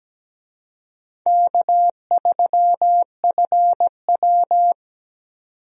Morse code, heard as a keyed tone, about 17 wpm, 700 Hz. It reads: K3FW
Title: USA Call Signs - 4 Characters - 17wpm